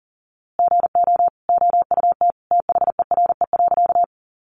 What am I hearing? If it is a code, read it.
Morse code, heard as a keyed tone, about 40 wpm, 700 Hz. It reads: GO OUT THERE.
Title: A Sound Of Thunder 40wpm